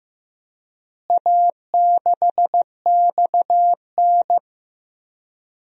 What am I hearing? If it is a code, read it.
Morse code, heard as a keyed tone, about 15 wpm, 700 Hz. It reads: A6XN
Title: DXCC-100 Call Signs - 4 Characters - Repeated 3x - 15wpm